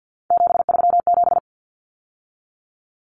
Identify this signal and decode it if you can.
Morse code, heard as a keyed tone, about 50 wpm, 700 Hz. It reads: 737
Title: Numbers - 3-Digit 50wpm